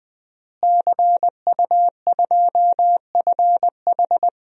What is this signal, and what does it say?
Morse code, heard as a keyed tone, about 20 wpm, 700 Hz. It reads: CU2FH
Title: DXCC-100 Call Signs - 5 Characters - Repeated 3x - 20wpm